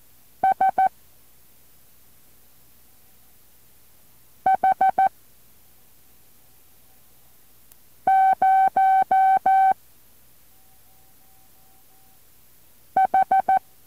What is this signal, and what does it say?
Morse code, heard as a keyed tone, about 14 wpm, 770 Hz. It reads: S H 0 H
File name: intl morse code2